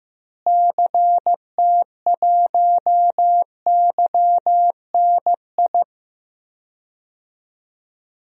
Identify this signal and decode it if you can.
Morse code, heard as a keyed tone, about 16 wpm, 700 Hz. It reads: CT1YNI
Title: DXCC-100 Call Signs - 6 Characters - Repeated 2x - 15wpm